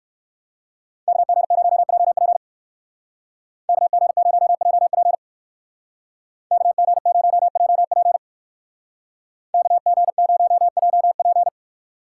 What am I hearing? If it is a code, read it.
Morse code, heard as a keyed tone, about 54 wpm, 700 Hz. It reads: KG0JP KG0JP KG0JP KG0JP
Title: Speed Racing - USA 2x2 Call Signs 45wpm